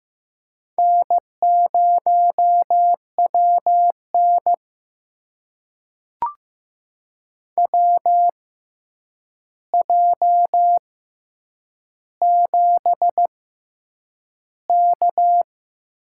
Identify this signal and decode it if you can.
Morse code, heard as a keyed tone, about 15 wpm, spaced as for 8 wpm, 700 Hz. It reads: N0WN WJ7K
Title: USA Call Signs - 4 Characters - 6x Character Spacing - 15wpm